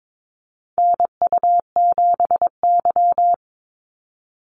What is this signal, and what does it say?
Morse code, heard as a keyed tone, about 22 wpm, 700 Hz. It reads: NU7Y
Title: NA SOTA Activator Callsigns 22wpm